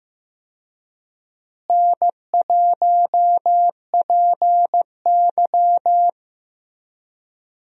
Morse code, heard as a keyed tone, about 15 wpm, 700 Hz. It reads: N1PY
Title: USA Call Signs - 4 Characters - Repeated 2x - 15wpm